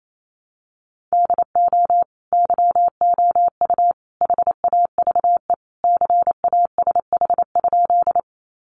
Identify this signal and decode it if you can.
Morse code, heard as a keyed tone, about 28 wpm, 700 Hz. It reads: DO YOU HAVE CASH?
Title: Sentences from Top 700 Words 28wpm